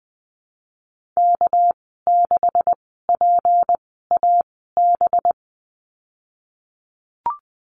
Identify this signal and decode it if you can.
Morse code, heard as a keyed tone, about 20 wpm, spaced as for 14 wpm, 700 Hz. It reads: K6PAB
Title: 213 Lesson 34g - Number 8 - Call Signs 20wpm